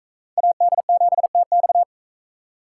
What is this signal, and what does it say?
Morse code, heard as a keyed tone, about 44 wpm, 700 Hz. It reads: AD7TX